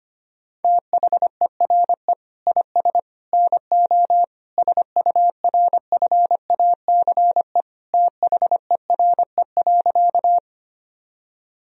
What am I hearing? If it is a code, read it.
Morse code, heard as a keyed tone, about 25 wpm, 700 Hz. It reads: THERE IS NO SURFACE THERE.
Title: Sentences from Top 1000 Words - Review Part 3 - 25wpm